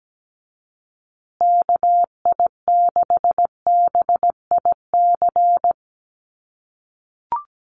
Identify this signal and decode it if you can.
Morse code, heard as a keyed tone, about 17 wpm, 700 Hz. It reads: KI6BIC
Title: USA Call Signs Repeated 3x 17wpm